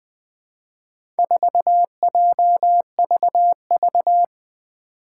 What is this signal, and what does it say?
Morse code, heard as a keyed tone, about 20 wpm, 700 Hz. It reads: 4JVV
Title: Four Challenging Letter-Number Combinations 20wpm